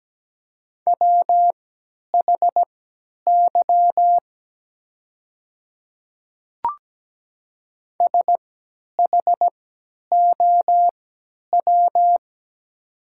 Morse code, heard as a keyed tone, about 17 wpm, spaced as for 10 wpm, 700 Hz. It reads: WHY SHOW
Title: 3x Character Spacing - Top 200 Words - 17wpm